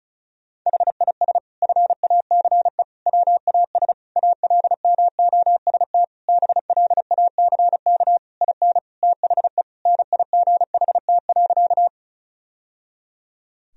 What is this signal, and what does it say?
Morse code, heard as a keyed tone, about 34 wpm, 700 Hz. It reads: HIS FACE WAS ALMOST BLACK IN THE NIGHT.
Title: Sentences from Top 1000 Words - Review Part 4 - 35wpm